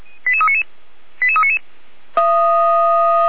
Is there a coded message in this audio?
Yes: five-tone selective call (ZVEI) from 0.3 to 1.6 s, heard as quick runs of five tones, tones of about 70 ms. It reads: 88399 88399